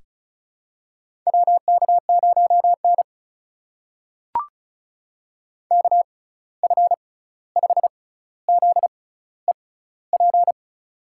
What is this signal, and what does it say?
Morse code, heard as a keyed tone, about 34 wpm, spaced as for 16 wpm, 700 Hz. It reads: WK0N KF5ZEP
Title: USA Call Signs - 6x Character Spacing - 35wpm